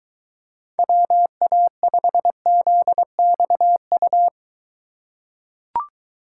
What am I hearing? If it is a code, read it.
Morse code, heard as a keyed tone, about 23 wpm, 700 Hz. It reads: WA5ZXU